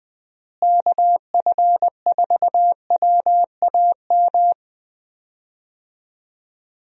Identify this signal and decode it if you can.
Morse code, heard as a keyed tone, about 20 wpm, 700 Hz. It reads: KF4WAM
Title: USA Call Signs - 6 Characters - Repeated 2x - 20wpm